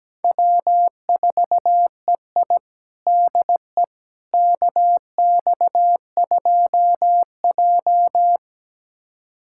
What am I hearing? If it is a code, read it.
Morse code, heard as a keyed tone, about 17 wpm, 700 Hz. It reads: W4EI DE KX2J
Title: USA Callsign DE Callsign - 4 Characters - 17wpm